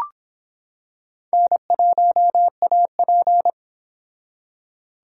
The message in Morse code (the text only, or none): N1AP